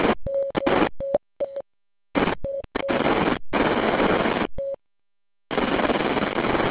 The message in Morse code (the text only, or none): none